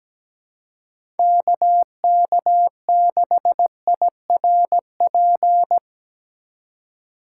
Morse code, heard as a keyed tone, about 17 wpm, 700 Hz. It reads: KK6IRP
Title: USA Call Signs - 8x Character Spacing - 17wpm